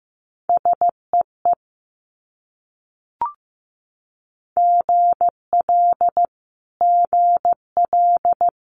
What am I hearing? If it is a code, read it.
Morse code, heard as a keyed tone, about 15 wpm, 700 Hz. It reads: SEE GL GL
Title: Multi-letter Phonemes Repeated 3x 15wpm